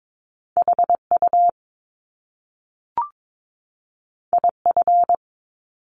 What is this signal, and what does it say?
Morse code, heard as a keyed tone, about 22 wpm, 700 Hz. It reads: HU IF
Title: Top Two Letter Combinations in Words 22wpm